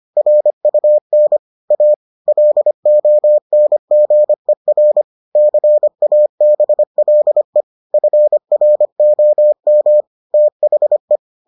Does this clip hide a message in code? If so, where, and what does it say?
Yes: Morse code from 0.2 to 11.2 s, heard as a keyed tone, about 25 wpm, 600 Hz. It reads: RUN A LONGER CABLE FROM THE